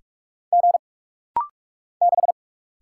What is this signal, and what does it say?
Morse code, heard as a keyed tone, about 46 wpm, 700 Hz. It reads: G 6
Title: Single Letter-Number - Rapid-Fire 45wpm